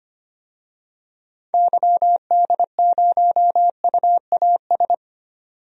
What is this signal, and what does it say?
Morse code, heard as a keyed tone, about 25 wpm, 700 Hz. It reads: YD0UAS
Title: DXCC-100 Call Signs - 6 Characters - 25wpm